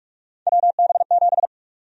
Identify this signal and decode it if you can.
Morse code, heard as a keyed tone, about 47 wpm, 700 Hz. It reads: WB7
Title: USA Call Sign Prefixes - 45wpm